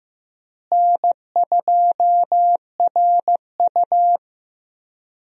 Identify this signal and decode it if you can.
Morse code, heard as a keyed tone, about 15 wpm, 700 Hz. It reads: N2RU
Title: USA Call Signs - 4 Characters - Repeated 3x - 15wpm